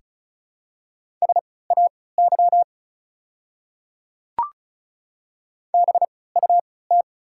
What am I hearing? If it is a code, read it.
Morse code, heard as a keyed tone, about 36 wpm, spaced as for 20 wpm, 700 Hz. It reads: SAY BUT
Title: 3x Character Spacing - Top 200 Words - 35wpm